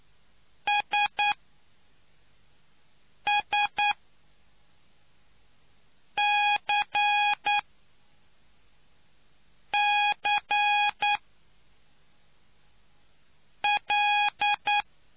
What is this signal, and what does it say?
Morse code, heard as a keyed tone, about 9 wpm, spaced as for 3 wpm, 830 Hz. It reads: SSCCL